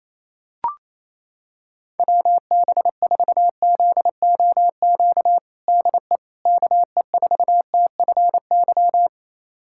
Morse code, heard as a keyed tone, about 28 wpm, 700 Hz. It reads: WB4ZOQ DE KE4TFY
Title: USA Callsign DE Callsign 28wpm